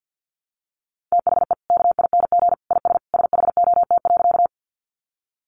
Morse code, heard as a keyed tone, about 50 wpm, 700 Hz. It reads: THE KING IS SHOT.
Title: Sentences from Top 1100 Words 50wpm